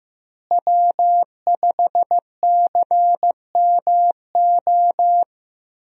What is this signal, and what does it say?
Morse code, heard as a keyed tone, about 15 wpm, 700 Hz. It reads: W5CMO
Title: USA Call Signs - 5 Characters - Repeated 2x - 15wpm